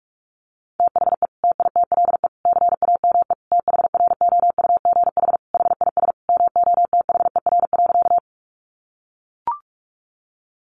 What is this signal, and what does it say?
Morse code, heard as a keyed tone, about 44 wpm, 700 Hz. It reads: THE TITLE CAME THROUGH HIS MOTHER.
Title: Sentences from Top 1200 Words 45wpm